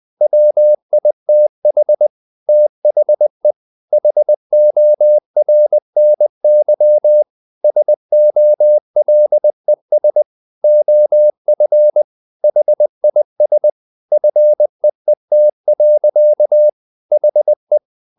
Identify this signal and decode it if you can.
Morse code, heard as a keyed tone, about 20 wpm, 600 Hz. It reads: WITH THE HORNY SOLES OF HIS FEET. HE
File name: the old man and the sea / the old man and the sea part 1 20wpm